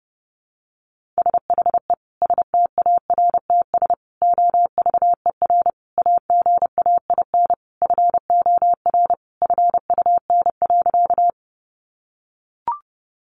Morse code, heard as a keyed tone, about 30 wpm, 700 Hz. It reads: SHE STARTS OVER AGAIN FOR FUN.
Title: Easy Phrases 30wpm